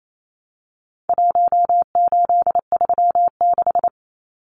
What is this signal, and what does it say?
Morse code, heard as a keyed tone, about 28 wpm, 700 Hz. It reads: 1836
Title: Numbers - Contest Style - 4-Digit 28wpm